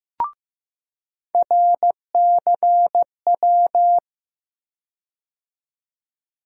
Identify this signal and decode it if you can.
Morse code, heard as a keyed tone, about 15 wpm, 700 Hz. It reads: RCW
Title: USA Call Sign Postfixes - 3 Letters 15wpm